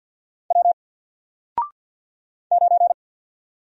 Morse code, heard as a keyed tone, about 48 wpm, 700 Hz. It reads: W 9
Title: Single Character (Letter:Number:Punctuation:Prosign) 50wpm